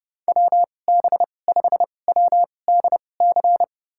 Morse code, heard as a keyed tone, about 30 wpm, spaced as for 22 wpm, 700 Hz. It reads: WB5WDC
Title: USA Call Signs - 6 Characters - 2x Character Spacing - 30wpm